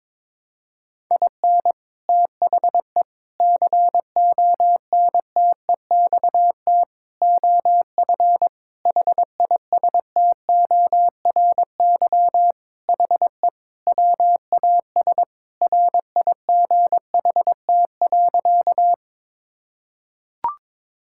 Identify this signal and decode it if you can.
Morse code, heard as a keyed tone, about 22 wpm, 700 Hz. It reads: IN THE CONTEXT OF HISTORY HE WAS RIGHT.
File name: practice / 22wpm / Sentences with 4-Word Phrases - 22wpm